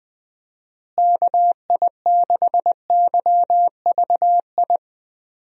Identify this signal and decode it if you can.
Morse code, heard as a keyed tone, about 20 wpm, 700 Hz. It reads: KI6YVI
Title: USA Call Signs - 6 Characters - Repeated 2x - 20wpm